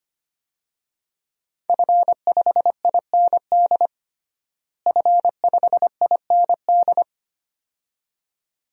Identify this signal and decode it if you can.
Morse code, heard as a keyed tone, about 25 wpm, 700 Hz. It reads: F5IND F5IND
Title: DXCC-100 Call Signs - 5 Characters - Repeated 2x - 25wpm